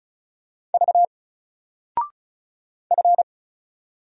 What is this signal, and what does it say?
Morse code, heard as a keyed tone, about 34 wpm, 700 Hz. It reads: V F
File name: Single-Letter-Number-35wpm